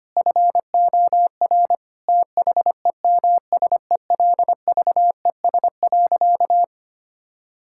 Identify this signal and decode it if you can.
Morse code, heard as a keyed tone, about 25 wpm, 700 Hz. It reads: FOR THEMSELVES.
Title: Sentences from Top 1000 Words - Review Part 5 - 25wpm